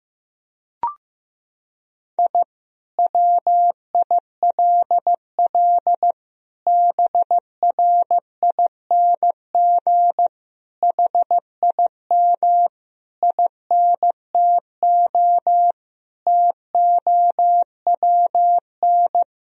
Morse code, heard as a keyed tone, about 16 wpm, 700 Hz. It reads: I WILL BRING HIM INTO TOWN
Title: Sentences from Top 1000 Words - Review Part 3 - 15wpm